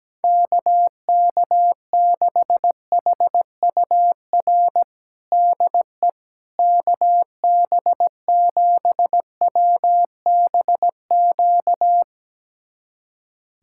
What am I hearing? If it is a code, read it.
Morse code, heard as a keyed tone, about 17 wpm, 700 Hz. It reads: KK6HUR DE KB7WBQ
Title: USA Callsign DE Callsign - 6 Characters - 17wpm